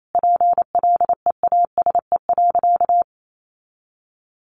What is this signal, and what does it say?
Morse code, heard as a keyed tone, about 28 wpm, 700 Hz. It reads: PLEASE.
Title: Sentences from Top 1500 Words 28wpm